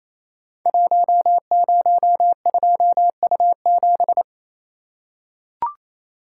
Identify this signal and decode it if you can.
Morse code, heard as a keyed tone, about 29 wpm, 700 Hz. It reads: 102U7
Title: Code Groups - 5 Letter:Numbers - 28wpm